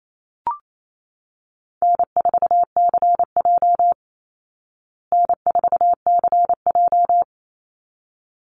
Morse code, heard as a keyed tone, about 28 wpm, 700 Hz. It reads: N4CJ N4CJ
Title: Easy Call Signs Repeated 2x - 28wpm